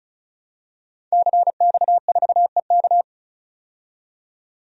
Morse code, heard as a keyed tone, about 34 wpm, 700 Hz. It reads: CX4EK
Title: DXCC-100 Call Signs - 5 Characters - 35wpm